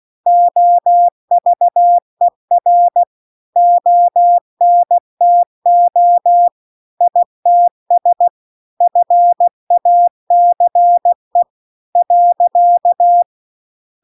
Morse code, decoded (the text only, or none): OVER ONTO ITS FACE .